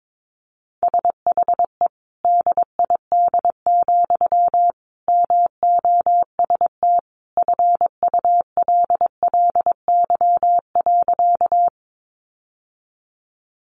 Morse code, heard as a keyed tone, about 22 wpm, 700 Hz. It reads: SHE DID, MOST FULLY.
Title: Sentences from Top 1300 Words 22wpm